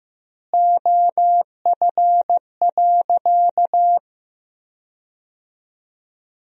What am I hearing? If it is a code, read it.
Morse code, heard as a keyed tone, about 15 wpm, 700 Hz. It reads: OF.
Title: 3x Word-Spacing - Sentences from Top 300 Words - 15wpm